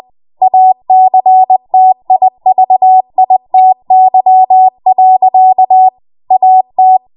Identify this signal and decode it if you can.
Morse code, heard as a keyed tone, about 20 wpm, 750 Hz. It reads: ACTIVITY. AT
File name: Morse / 20WPM